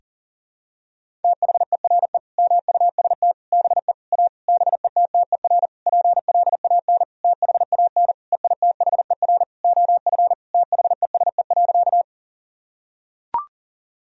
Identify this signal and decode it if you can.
Morse code, heard as a keyed tone, about 40 wpm, 700 Hz. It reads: THERE MUST BE A BETTER PLAN THAN EITHER OF THESE.
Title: Sentences from Top 1000 Words - Review Part 4 - 40wpm